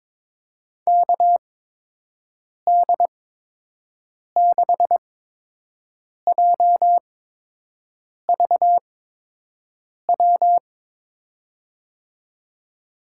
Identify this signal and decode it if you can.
Morse code, heard as a keyed tone, about 23 wpm, spaced as for 6 wpm, 700 Hz. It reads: KD6JVW
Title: USA Call Signs - 6 Characters - 8x Character Spacing - 22wpm